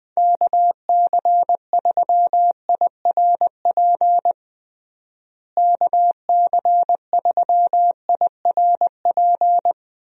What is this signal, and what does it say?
Morse code, heard as a keyed tone, about 20 wpm, 700 Hz. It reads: KC3IRP KC3IRP
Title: USA Call Signs - 6 Characters - Repeated 3x - 20wpm